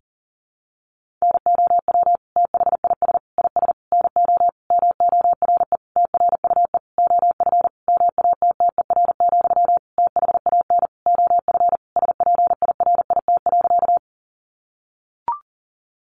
Morse code, heard as a keyed tone, about 37 wpm, 700 Hz. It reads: NOW THIS IS NO MORE TRUE OF MATTER, THAN OF SPIRIT.
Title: Sentences from Top 1400 Words 40wpm